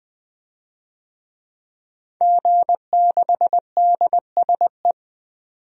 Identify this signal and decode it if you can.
Morse code, heard as a keyed tone, about 21 wpm, 700 Hz. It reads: G6DSE
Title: DXCC-100 Call Signs - 5 Characters - 20wpm